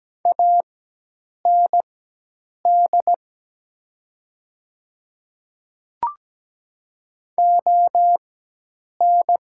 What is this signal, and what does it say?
Morse code, heard as a keyed tone, about 17 wpm, spaced as for 8 wpm, 700 Hz. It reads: AND ON
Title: 4x Character Spacing - Top 100 Words - 17wpm